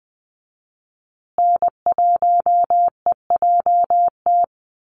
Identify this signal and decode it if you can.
Morse code, heard as a keyed tone, about 20 wpm, 700 Hz. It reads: N1EJT
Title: USA Call Signs Repeated 2x 20wpm